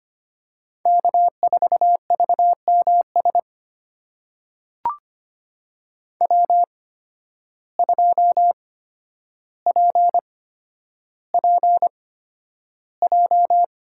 Morse code, heard as a keyed tone, about 25 wpm, spaced as for 11 wpm, 700 Hz. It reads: K4VMS W2PPJ
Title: USA Call Signs - 5 Characters - 8x Character Spacing - 25wpm